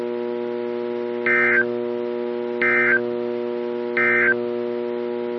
Emergency Alert System (SAME) tones, heard as harsh buzzing bursts. It is heard from 1.2 to 4.4 s.
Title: NOAA Weather Radio - KEC55